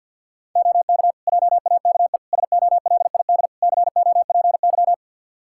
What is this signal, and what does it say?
Morse code, heard as a keyed tone, about 52 wpm, 700 Hz. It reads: OK JAKE SOLID COPY
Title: Practice QSO - Exchange 3 - More Info - 50wpm